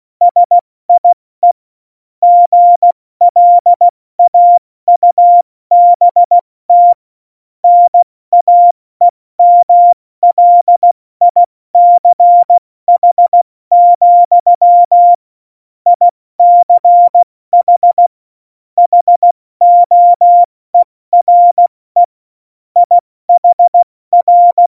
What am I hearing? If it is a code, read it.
Morse code, heard as a keyed tone, about 16 wpm, 700 Hz. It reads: SIE GLAUBT NAEMLICH, ICH HOERE IHR
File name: Morsekurs / Witze / Witz 23. Juli 2019, 70 F80